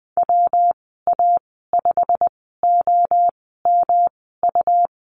Morse code, heard as a keyed tone, about 20 wpm, spaced as for 15 wpm, 700 Hz. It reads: WA5OMU
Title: 122 Lesson 20g - Number 3 - Call Signs 20wpm